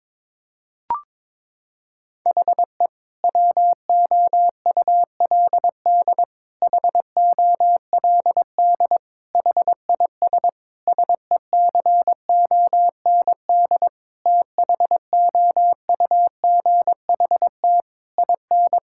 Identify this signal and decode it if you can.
Morse code, heard as a keyed tone, about 22 wpm, 700 Hz. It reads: HE WOULD HOLD HIS SECOND THOUGHT IN
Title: Sentences from Top 1000 Words - Review Part 6 - 22wpm